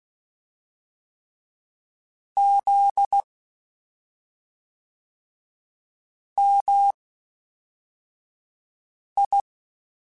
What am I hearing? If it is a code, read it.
Morse code, heard as a keyed tone, about 16 wpm, spaced as for 3 wpm, 800 Hz. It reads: ZMI
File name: CW Comparisons / CW Comparisons-mp3s / CW Comparisons Broken Characters @5